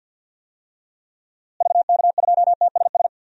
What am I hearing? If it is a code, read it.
Morse code, heard as a keyed tone, about 51 wpm, 700 Hz. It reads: VK2TSS
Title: DXCC-100 Call Signs - 6 Characters - Repeated 2x - 50wpm